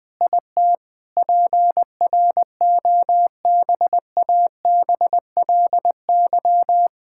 Morse code, heard as a keyed tone, about 20 wpm, 700 Hz. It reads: IT PROBABLY
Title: Sets of 9 Words - 20wpm